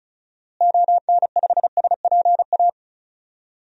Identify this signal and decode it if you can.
Morse code, heard as a keyed tone, about 34 wpm, 700 Hz. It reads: ON5SPA
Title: DXCC-100 Call Signs - 6 Characters - Repeated 2x - 35wpm